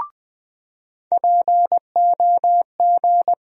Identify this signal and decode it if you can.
Morse code, heard as a keyed tone, about 20 wpm, 700 Hz. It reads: POG
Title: USA Call Sign Postfixes - 3 Letters 20wpm